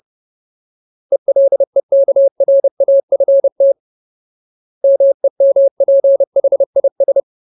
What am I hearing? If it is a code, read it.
Morse code, heard as a keyed tone, about 30 wpm, 550 Hz. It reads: ELEKRAFT MEMPHIS
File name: Ultimate Word List / 30WPM Ultimate